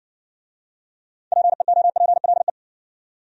Morse code, heard as a keyed tone, about 60 wpm, 700 Hz. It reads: PEOPLE